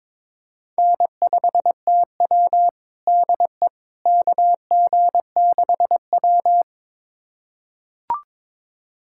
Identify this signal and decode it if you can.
Morse code, heard as a keyed tone, about 21 wpm, 700 Hz. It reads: N5TW DE KG6W
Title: USA Callsign DE Callsign - 4 Characters - 22wpm